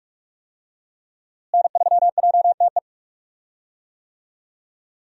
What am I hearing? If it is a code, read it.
Morse code, heard as a keyed tone, about 42 wpm, 700 Hz. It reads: N3JTE
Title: USA Call Signs - 5 Characters - Repeated 3x - 45wpm